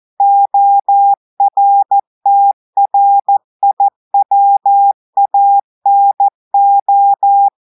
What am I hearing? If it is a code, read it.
Morse code, heard as a keyed tone, about 14 wpm, 800 Hz. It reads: ORTRIWANO
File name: audios / portugais / portugais texte en clair 1 vitesse 14 mots minutes